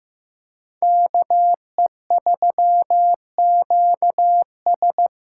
Morse code, heard as a keyed tone, about 15 wpm, 700 Hz. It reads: KE3QS
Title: USA Callsign DE Callsign - 5 Characters - 15wpm